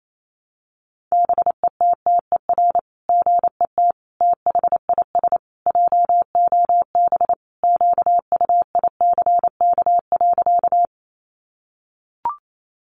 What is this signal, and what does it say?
Morse code, heard as a keyed tone, about 28 wpm, 700 Hz. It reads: BETTER GET THIS JOB QUICK.
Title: Sentences from Top 600 Words 28wpm